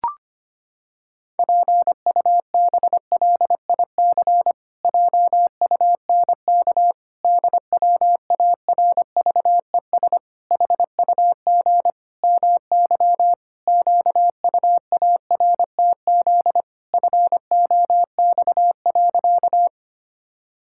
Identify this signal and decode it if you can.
Morse code, heard as a keyed tone, about 25 wpm, 700 Hz. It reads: PUBLIC JUNK DWARVES HUG MY QUARTZ FOX.
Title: Pangram Challenge v2 - 25wpm